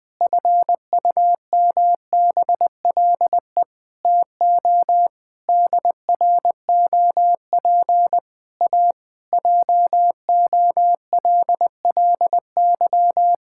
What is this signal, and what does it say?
Morse code, heard as a keyed tone, about 20 wpm, 700 Hz. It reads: FUMBLE TO DROP A JOLLY
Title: Pangram Challenge v2 - 20wpm